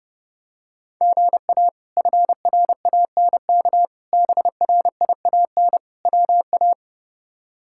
Morse code, heard as a keyed tone, about 30 wpm, 700 Hz. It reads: GA FRANK BRIAN WA
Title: K1USN SST QSO Practice - 30wpm